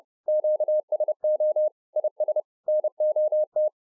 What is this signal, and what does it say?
Morse code, heard as a keyed tone, about 30 wpm, 610 Hz. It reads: QSO IS NOT